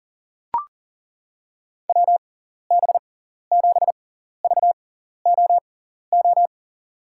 Morse code, heard as a keyed tone, about 40 wpm, spaced as for 14 wpm, 690 Hz. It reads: WB7VOO
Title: USA Call Signs - 6x Character Spacing - 40wpm